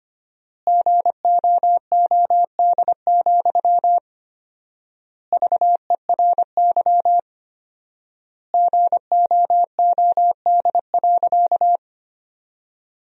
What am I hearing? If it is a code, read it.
Morse code, heard as a keyed tone, about 25 wpm, 700 Hz. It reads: GOOD, VERY GOOD.
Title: 4x Word-Spacing - Sentences from Top 200 Words - 25wpm